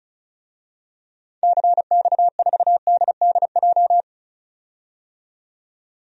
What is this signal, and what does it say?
Morse code, heard as a keyed tone, about 36 wpm, 700 Hz. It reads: CX4DDJ
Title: DXCC-100 Call Signs - 6 Characters - 35wpm